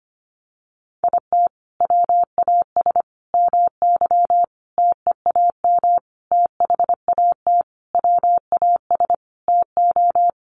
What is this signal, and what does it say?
Morse code, heard as a keyed tone, about 25 wpm, 700 Hz. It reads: IT WAS MY TEAM THAT WAS TO
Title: Sentences from Top 1500 Words 25wpm